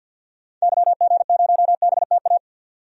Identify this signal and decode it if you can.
Morse code, heard as a keyed tone, about 50 wpm, 700 Hz. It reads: YG0BTA
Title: DXCC-100 Call Signs - 6 Characters - 50wpm